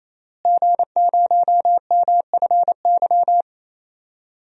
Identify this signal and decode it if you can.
Morse code, heard as a keyed tone, about 28 wpm, 700 Hz. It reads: G0MFY
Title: DXCC-100 Call Signs - 5 Characters - Repeated 3x - 28wpm